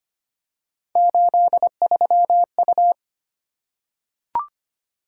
Morse code, heard as a keyed tone, about 25 wpm, 700 Hz. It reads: 83U
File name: Three Letter-Number Combinations - 25wpm